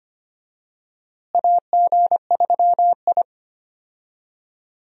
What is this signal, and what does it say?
Morse code, heard as a keyed tone, about 25 wpm, 700 Hz. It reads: AG3I
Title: USA Call Signs - 4 Characters - Repeated 2x - 25wpm